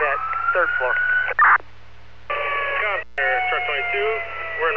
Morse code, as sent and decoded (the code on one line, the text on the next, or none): none